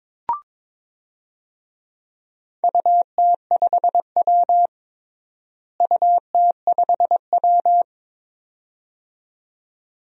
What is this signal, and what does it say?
Morse code, heard as a keyed tone, about 22 wpm, 700 Hz. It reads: UT5W UT5W
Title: DXCC-100 Call Signs - 4 Characters - Repeated 2x - 22wpm